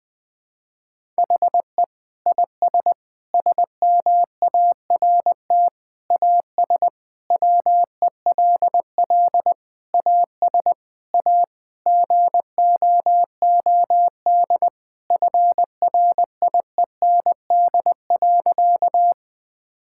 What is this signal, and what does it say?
Morse code, heard as a keyed tone, about 20 wpm, 700 Hz. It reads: HE IS SMART AS WELL AS A GOOD FRIEND.
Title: Sentences with 4-Word Phrases - 20wpm